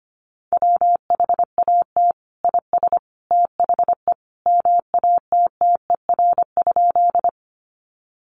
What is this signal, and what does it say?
Morse code, heard as a keyed tone, about 25 wpm, 700 Hz. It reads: WHAT IS THE MATTER?
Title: Sentences from Top 700 Words 25wpm